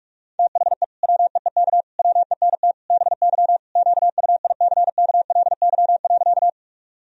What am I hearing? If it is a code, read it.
Morse code, heard as a keyed tone, about 41 wpm, 700 Hz. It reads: THE WEEK WENT BY QUICKLY.
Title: Sentences from Top 1000 Words - Review Part 2 - 45wpm